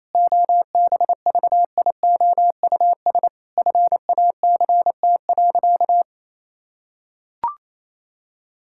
Morse code, heard as a keyed tone, about 28 wpm, 700 Hz. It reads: OBVIOUS FACT.
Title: Sentences from Top 1600 Words 28wpm